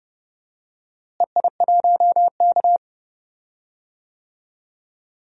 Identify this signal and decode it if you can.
Morse code, heard as a keyed tone, about 30 wpm, 700 Hz. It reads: EI1K
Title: DXCC-100 Call Signs - 4 Characters - Repeated 3x - 30wpm